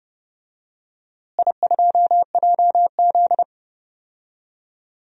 Morse code, heard as a keyed tone, about 30 wpm, 700 Hz. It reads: I2JZ
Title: DXCC-100 Call Signs - 4 Characters - Repeated 3x - 30wpm